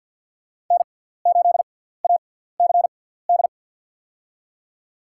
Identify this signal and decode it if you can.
Morse code, heard as a keyed tone, about 49 wpm, spaced as for 17 wpm, 700 Hz. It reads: N8ACD
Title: USA Call Signs - 5 Characters - 6x Character Spacing - 50wpm